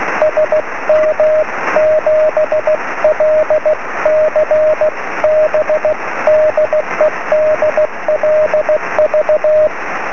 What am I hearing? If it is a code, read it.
Morse code, heard as a keyed tone, about 16 wpm, 620 Hz, with radio noise. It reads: SM7LCB DE DLV
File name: sm7lcb / 20 23cm / 2219 dl4ebv 23cm